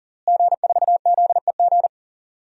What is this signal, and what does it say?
Morse code, heard as a keyed tone, about 38 wpm, 700 Hz. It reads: G4ZEG
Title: DXCC-100 Call Signs - 5 Characters - Repeated 3x - 40wpm